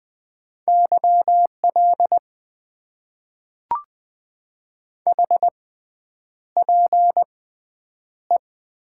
Morse code, heard as a keyed tone, about 20 wpm, spaced as for 9 wpm, 700 Hz. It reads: YL HPE
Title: Top 90 QSO Elements - 6x Character Spacing - 20wpm